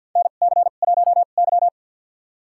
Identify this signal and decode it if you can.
Morse code, heard as a keyed tone, about 51 wpm, spaced as for 36 wpm, 700 Hz. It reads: NC1Y
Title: USA Call Signs - 4 Characters - 2x Character Spacing - 50wpm